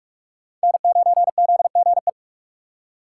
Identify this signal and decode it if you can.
Morse code, heard as a keyed tone, about 43 wpm, 690 Hz. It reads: N9ZGE
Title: USA Call Signs - 5 Characters - 2x Character Spacing - 45wpm